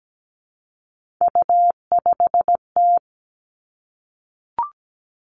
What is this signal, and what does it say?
Morse code, heard as a keyed tone, about 17 wpm, 700 Hz. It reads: U5T
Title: Three Challenging Letter-Number Combinations 17wpm